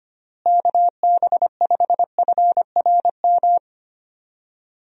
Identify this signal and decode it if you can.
Morse code, heard as a keyed tone, about 25 wpm, 700 Hz. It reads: KB5FRM
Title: USA Call Signs - 6 Characters - Repeated 2x - 25wpm